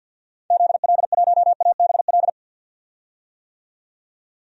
Morse code, heard as a keyed tone, about 50 wpm, 700 Hz. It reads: ZL1ABL
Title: DXCC-100 Call Signs - 6 Characters - 50wpm